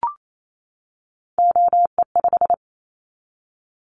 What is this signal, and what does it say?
Morse code, heard as a keyed tone, about 28 wpm, 700 Hz. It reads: OE5
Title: Callsign Prefixes for Top 50 DXCC - 28wpm